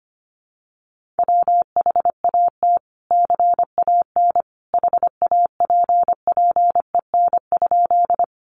Morse code, heard as a keyed tone, about 25 wpm, 700 Hz. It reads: WHAT CAN HAPPEN?
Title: Sentences from Top 400 Words 25wpm